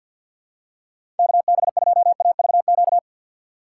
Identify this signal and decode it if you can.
Morse code, heard as a keyed tone, about 53 wpm, 690 Hz. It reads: KB2AVQ
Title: USA Call Signs - 6 Characters - 8x Character Spacing - 50wpm